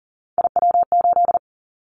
Morse code, heard as a keyed tone, about 40 wpm, 700 Hz. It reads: IW8
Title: Callsign Prefixes for Top 25 DXCC - Repeated 3x - 40wpm